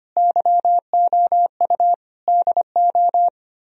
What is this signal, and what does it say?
Morse code, heard as a keyed tone, about 32 wpm, 700 Hz. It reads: YOU DO